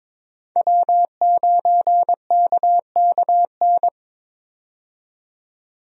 Morse code, heard as a keyed tone, about 22 wpm, 700 Hz. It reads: W9KKN
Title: USA Call Signs - 5 Characters - 22wpm